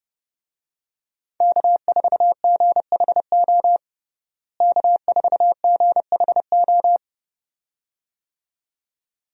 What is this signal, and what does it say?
Morse code, heard as a keyed tone, about 32 wpm, 700 Hz. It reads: K4GHO K4GHO
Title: USA Call Signs - 5 Characters - Repeated 2x - 30wpm